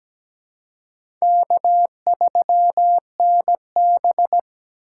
Morse code, heard as a keyed tone, about 17 wpm, 700 Hz. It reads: K3NB
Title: USA Call Signs - 4 Characters - 4x Character Spacing - 17wpm